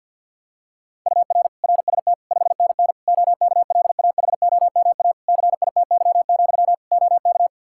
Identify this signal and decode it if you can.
Morse code, heard as a keyed tone, about 47 wpm, 700 Hz. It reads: UR RST 5NN OKLAHOMA CITY, OK
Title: Practice QSO - Exchange 1 - Beginning - 50wpm